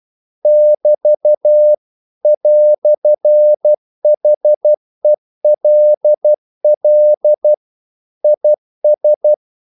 Morse code, heard as a keyed tone, about 12 wpm, 600 Hz. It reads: <BT> "HELL IS
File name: CW / Practice / EnglishQuotations0312